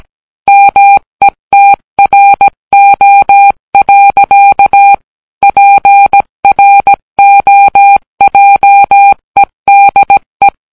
Morse code, heard as a keyed tone, about 17 wpm, 800 Hz. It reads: METRO. PROJEDE